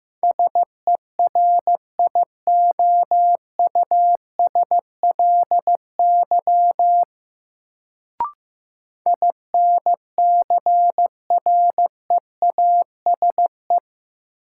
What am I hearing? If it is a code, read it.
Morse code, heard as a keyed tone, about 15 wpm, 700 Hz. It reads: SERIOUSLY INCREASE
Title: Sentences from Top 1200 Words - New Word Drill Alt - 15wpm